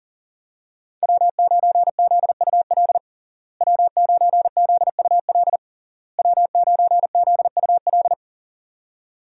Morse code, heard as a keyed tone, about 37 wpm, 700 Hz. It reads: W9ZUL W9ZUL W9ZUL
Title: USA Call Signs - 5 Characters - Repeated 3x - 40wpm